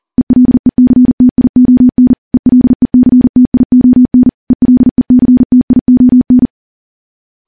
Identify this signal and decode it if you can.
Morse code, heard as a keyed tone, about 40 wpm, 250 Hz. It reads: ELECTION ELECTION ELECTION
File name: morse / morse-training-40WPM